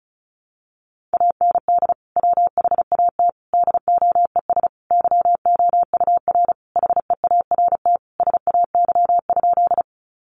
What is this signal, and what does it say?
Morse code, heard as a keyed tone, about 34 wpm, 700 Hz. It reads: AND WHAT DOES YOUR HEART SAY?
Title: Sentences from Top 1200 Words 35wpm